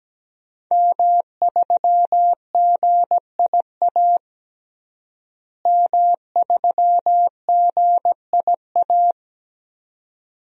Morse code, heard as a keyed tone, about 17 wpm, 700 Hz. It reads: M3GIA M3GIA
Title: DXCC-100 Call Signs - 5 Characters - Repeated 2x - 17wpm